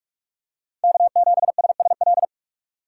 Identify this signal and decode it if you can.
Morse code, heard as a keyed tone, about 46 wpm, 700 Hz. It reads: K7SSL
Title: USA Call Signs - 5 Characters - 4x Character Spacing - 45wpm